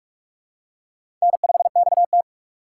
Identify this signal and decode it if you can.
Morse code, heard as a keyed tone, about 44 wpm, 700 Hz. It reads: N5XT
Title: USA Call Signs - 4 Characters - 6x Character Spacing - 45wpm